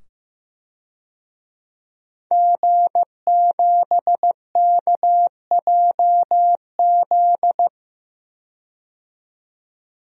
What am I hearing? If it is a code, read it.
Morse code, heard as a keyed tone, about 15 wpm, 700 Hz. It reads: G7KJZ
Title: DXCC-100 Call Signs - 5 Characters - 15wpm